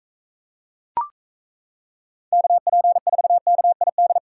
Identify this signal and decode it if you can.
Morse code, heard as a keyed tone, about 35 wpm, 700 Hz. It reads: KP4KID